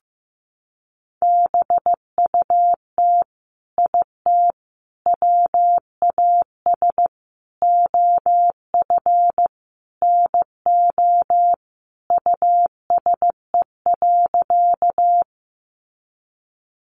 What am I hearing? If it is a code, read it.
Morse code, heard as a keyed tone, about 15 wpm, 700 Hz. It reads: BUT IT WAS OF NO USE.
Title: Sentences from Top 100 Words 15wpm